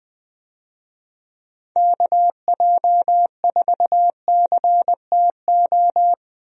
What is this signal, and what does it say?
Morse code, heard as a keyed tone, about 21 wpm, 700 Hz. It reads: KJ4CTO
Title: USA Call Signs - 6 Characters - Repeated 2x - 20wpm